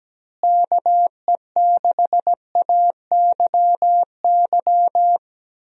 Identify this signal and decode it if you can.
Morse code, heard as a keyed tone, about 17 wpm, 700 Hz. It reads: KE6AYY
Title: USA Call Signs - 6 Characters - Repeated 2x - 17wpm